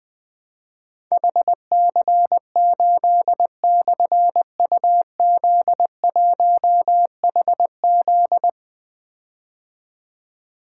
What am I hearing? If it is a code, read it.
Morse code, heard as a keyed tone, about 20 wpm, 700 Hz. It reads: HC8/UZ1HZ
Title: Difficult Call Signs 20wpm